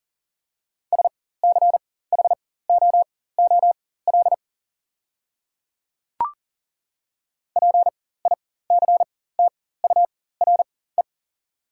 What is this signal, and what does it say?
Morse code, heard as a keyed tone, about 37 wpm, spaced as for 18 wpm, 690 Hz. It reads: SCHOOL PICTURE
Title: 4x Character Spacing - Top 500 Words - 40wpm